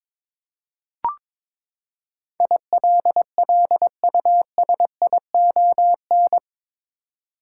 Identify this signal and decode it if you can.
Morse code, heard as a keyed tone, about 22 wpm, 700 Hz. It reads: ILLUSION
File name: practice / 22wpm / Eight Letter Words - 22wpm